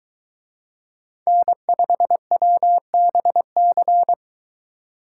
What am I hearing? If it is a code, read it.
Morse code, heard as a keyed tone, about 23 wpm, 700 Hz. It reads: N5WBC